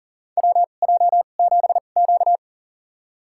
Morse code, heard as a keyed tone, about 37 wpm, spaced as for 28 wpm, 700 Hz. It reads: WJ7Q
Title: USA Call Signs - 4 Characters - 2x Character Spacing - 40wpm